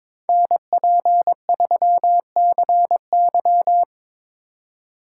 Morse code, heard as a keyed tone, about 26 wpm, 700 Hz. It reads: NP3CY
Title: Speed Racing - USA 2x2 Call Signs 22wpm